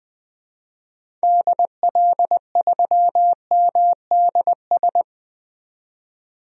Morse code, heard as a keyed tone, about 20 wpm, 700 Hz. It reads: DL3MDS
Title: DXCC-100 Call Signs - 6 Characters - 20wpm